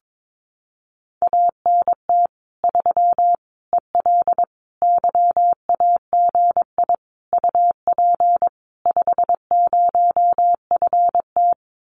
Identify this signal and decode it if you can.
Morse code, heard as a keyed tone, about 22 wpm, 700 Hz. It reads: ANT 3 EL YAGI UP 50FT
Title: Common QSO Phrases Encore 22wpm